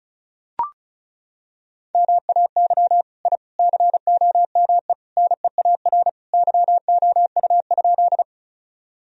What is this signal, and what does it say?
Morse code, heard as a keyed tone, about 36 wpm, 700 Hz. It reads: MAY I COME NEAR YOU?
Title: Sentences from Top 1000 Words - Review Part 1 - 35wpm